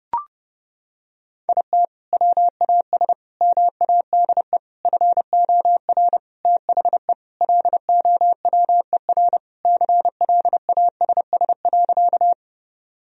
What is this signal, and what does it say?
Morse code, heard as a keyed tone, about 30 wpm, 700 Hz. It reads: IT WAS MADE FOR THE LOWER CLASS.
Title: Sentences from Top 1000 Words - Review Part 4 - 30wpm